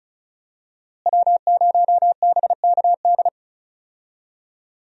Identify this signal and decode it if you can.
Morse code, heard as a keyed tone, about 36 wpm, 700 Hz. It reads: W0BKD
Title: USA Call Signs - 5 Characters - Repeated 2x - 35wpm